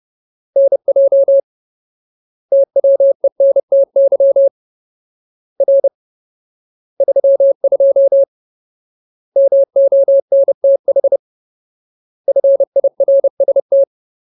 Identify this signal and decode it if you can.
Morse code, heard as a keyed tone, about 30 wpm, 550 Hz. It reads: NJ TWENTY R 32 MONTH FIRST
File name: Ultimate Word List / 30WPM Ultimate